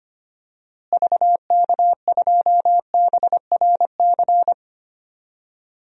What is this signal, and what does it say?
Morse code, heard as a keyed tone, about 25 wpm, 700 Hz. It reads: VK2BRC
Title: DXCC-100 Call Signs - 6 Characters - Repeated 3x - 25wpm